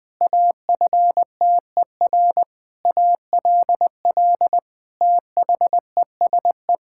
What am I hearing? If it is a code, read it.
Morse code, heard as a keyed tone, about 20 wpm, 700 Hz. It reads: AFTER ALL THESE
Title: Sentences from Top 1000 Words - Review Part 6 - 20wpm